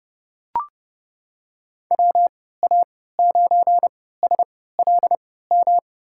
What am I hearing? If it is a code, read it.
Morse code, heard as a keyed tone, about 30 wpm, spaced as for 17 wpm, 700 Hz. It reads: WA9SLM
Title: USA Call Signs - 3x Character Spacing - 30wpm